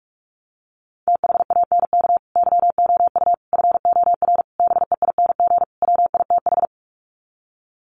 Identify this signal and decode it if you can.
Morse code, heard as a keyed tone, about 44 wpm, 700 Hz. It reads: THANK YOU FOR BEING WITH
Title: Sets of 5 Words 45wpm